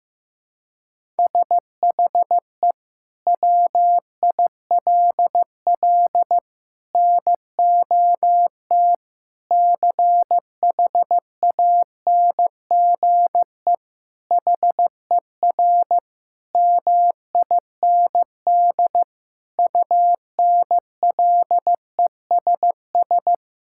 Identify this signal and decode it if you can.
Morse code, heard as a keyed tone, about 16 wpm, 700 Hz. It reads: SHE WILL NOT CHANGE HER MIND UNLESS
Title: Sentences from Top 1000 Words - Review Part 6 - 15wpm